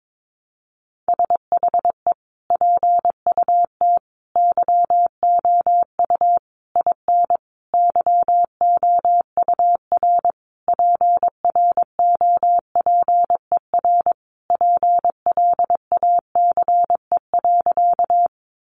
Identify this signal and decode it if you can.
Morse code, heard as a keyed tone, about 22 wpm, 700 Hz. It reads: SHE PUT YOU IN YOUR PROPER PLACE.
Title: Sentences from Top 800 Words 22wpm